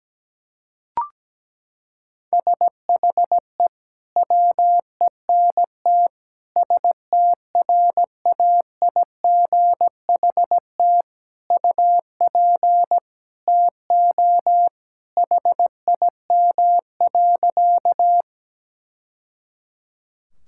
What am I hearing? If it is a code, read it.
Morse code, heard as a keyed tone, about 17 wpm, 700 Hz. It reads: SHE WENT STRAIGHT UP TO HIM.
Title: Sentences from Top 1000 Words - Review Part 6 - 17wpm